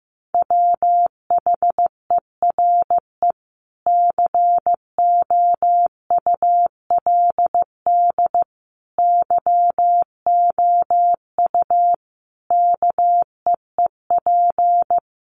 Morse code, heard as a keyed tone, about 15 wpm, 700 Hz. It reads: WHERE COULD YOU KEEP
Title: Sentences from Top 300 Words 15wpm